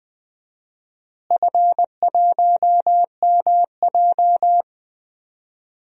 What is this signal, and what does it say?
Morse code, heard as a keyed tone, about 21 wpm, 700 Hz. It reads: F1MJ
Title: DXCC-100 Call Signs - 4 Characters - Repeated 3x - 20wpm